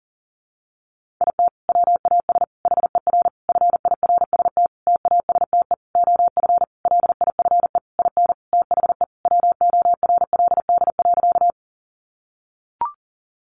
Morse code, heard as a keyed tone, about 40 wpm, 700 Hz. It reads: IT WAS HER FIRST TASTE OF LIFE IN THE WORLD.
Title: Sentences from Top 1100 Words 40wpm